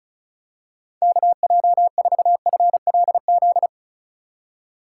Morse code, heard as a keyed tone, about 36 wpm, 700 Hz. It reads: KJ4FLZ
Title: USA Call Signs - 6 Characters - 2x Character Spacing - 35wpm